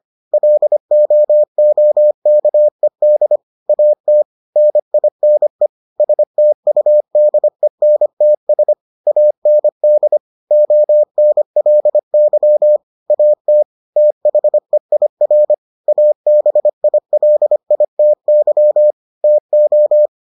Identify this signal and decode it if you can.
Morse code, heard as a keyed tone, about 25 wpm, 600 Hz. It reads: LOOKED AT NINE STUDENTS AND ONLY AT THEIR ABILITY TO